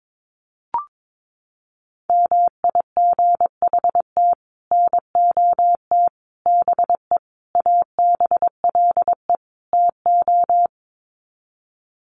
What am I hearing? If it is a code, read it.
Morse code, heard as a keyed tone, about 22 wpm, 700 Hz. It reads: MIGHT NOT BE ABLE TO
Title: Sets of 5 Words Encore2 22wpm